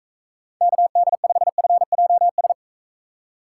Morse code, heard as a keyed tone, about 37 wpm, 700 Hz. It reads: KD5FJS